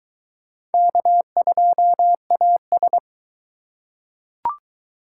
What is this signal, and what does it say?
Morse code, heard as a keyed tone, about 20 wpm, 700 Hz. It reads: K2AS